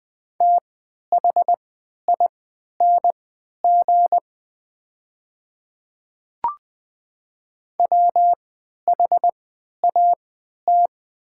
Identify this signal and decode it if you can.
Morse code, heard as a keyed tone, about 20 wpm, spaced as for 11 wpm, 700 Hz. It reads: THING WHAT
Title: 3x Character Spacing - Top 200 Words - 20wpm